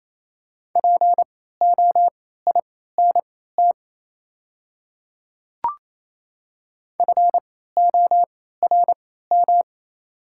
Morse code, heard as a keyed tone, about 28 wpm, spaced as for 16 wpm, 700 Hz. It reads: POINT FORM
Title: 3x Character Spacing - Top 200 Words - 28wpm